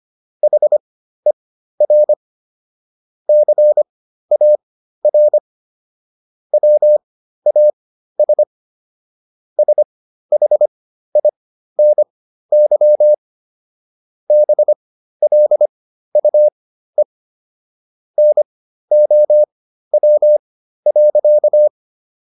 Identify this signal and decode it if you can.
Morse code, heard as a keyed tone, about 25 wpm, spaced as for 13 wpm, 600 Hz. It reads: HER CAR WAS SHINY BLUE NOW.